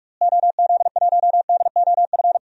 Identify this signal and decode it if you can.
Morse code, heard as a keyed tone, about 46 wpm, 700 Hz. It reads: OZ1DOF
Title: DXCC-100 Call Signs - 6 Characters - Repeated 2x - 45wpm